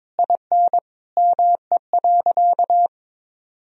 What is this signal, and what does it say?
Morse code, heard as a keyed tone, about 22 wpm, 700 Hz. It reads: IN ME.
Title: Sentences from Top 1000 Words - Review Part 3 - 22wpm